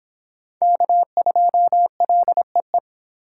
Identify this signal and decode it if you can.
Morse code, heard as a keyed tone, about 26 wpm, 700 Hz. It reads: K2LEE